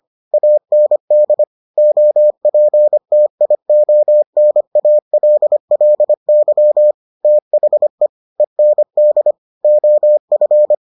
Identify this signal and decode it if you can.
Morse code, heard as a keyed tone, about 25 wpm, 600 Hz. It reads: AND OPTIONALLY THE END OF